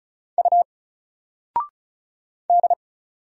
Morse code, heard as a keyed tone, about 34 wpm, 700 Hz. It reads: U D
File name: Single-Letter-Number-35wpm